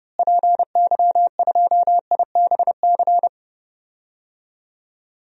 Morse code, heard as a keyed tone, about 31 wpm, 700 Hz. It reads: PY2IBC
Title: DXCC-100 Call Signs - 6 Characters - Repeated 3x - 30wpm